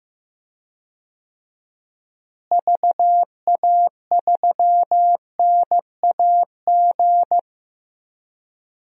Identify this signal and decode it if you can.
Morse code, heard as a keyed tone, about 15 wpm, 700 Hz. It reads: VA3NAG
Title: DXCC-100 Call Signs - 6 Characters - Repeated 3x - 15wpm